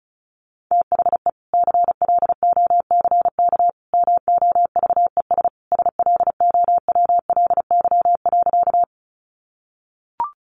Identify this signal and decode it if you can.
Morse code, heard as a keyed tone, about 34 wpm, 700 Hz. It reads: THE CLOCK MOVES SLOWLY.
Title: Easy Phrases 35wpm